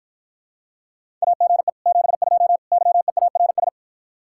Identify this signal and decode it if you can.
Morse code, heard as a keyed tone, about 54 wpm, 700 Hz. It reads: AGE 72 YEARS